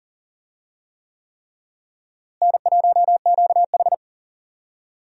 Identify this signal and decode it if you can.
Morse code, heard as a keyed tone, about 40 wpm, 700 Hz. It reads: N1QH
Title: USA Call Signs - 4 Characters - Repeated 3x - 40wpm